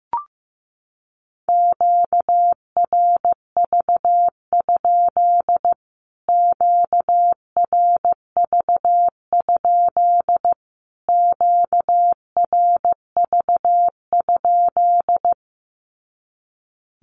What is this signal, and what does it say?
Morse code, heard as a keyed tone, about 15 wpm, 700 Hz. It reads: QRV? QRV? QRV?
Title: Q-Codes Repeated 3x 15wpm